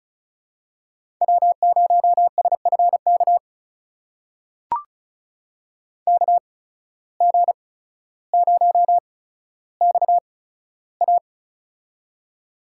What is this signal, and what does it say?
Morse code, heard as a keyed tone, about 36 wpm, spaced as for 15 wpm, 700 Hz. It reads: W0SFK KG0XA
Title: USA Call Signs - 5 Characters - 8x Character Spacing - 35wpm